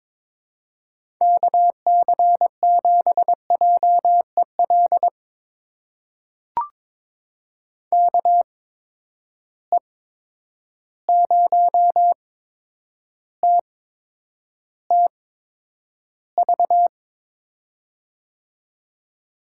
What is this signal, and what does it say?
Morse code, heard as a keyed tone, about 22 wpm, spaced as for 9 wpm, 700 Hz. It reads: KC7JEL KE0TTV
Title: USA Call Signs - 6 Characters - 8x Character Spacing - 22wpm